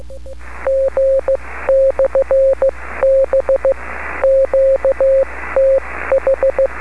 Morse code, heard as a keyed tone, about 15 wpm, spaced as for 13 wpm, 540 Hz. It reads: G/B QTH
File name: beacons / kd4hlg-b